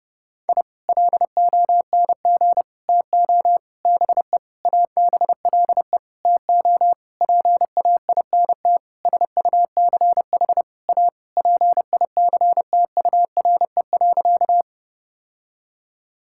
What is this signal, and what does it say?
Morse code, heard as a keyed tone, about 30 wpm, 700 Hz. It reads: I LONG TO BE ABLE TO PAINT SUCH A PICTURE.
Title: Sentences from Top 1000 Words - Review Part 6 - 30wpm